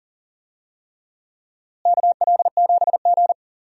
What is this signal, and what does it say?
Morse code, heard as a keyed tone, about 40 wpm, 700 Hz. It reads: KL7G
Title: DXCC-100 Call Signs - 4 Characters - Repeated 3x - 40wpm